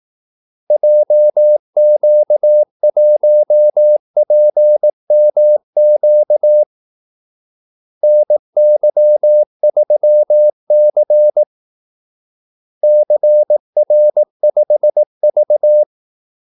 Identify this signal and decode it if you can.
Morse code, heard as a keyed tone, about 18 wpm, 600 Hz. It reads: JQ1PMQ NY3C CR5V
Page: Kaldesignaler 18 wpm